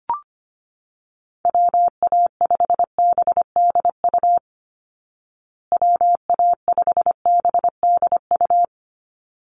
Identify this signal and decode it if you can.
Morse code, heard as a keyed tone, about 25 wpm, 700 Hz. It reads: WA5BDU WA5BDU
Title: Call Signs Repeated 3x - 25wpm